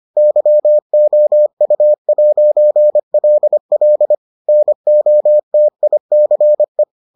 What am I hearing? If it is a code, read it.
Morse code, heard as a keyed tone, about 25 wpm, 600 Hz. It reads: YOU'LL NOTICE